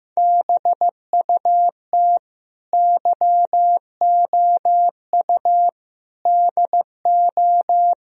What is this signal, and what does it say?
Morse code, heard as a keyed tone, about 16 wpm, 700 Hz. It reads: BUT YOU DO
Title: Speed-Racing - Sets of 4 Words 15wpm